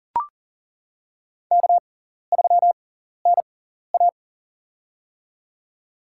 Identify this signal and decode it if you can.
Morse code, heard as a keyed tone, about 39 wpm, spaced as for 14 wpm, 700 Hz. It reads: K3NA
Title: USA Call Signs - 4 Characters - 6x Character Spacing - 40wpm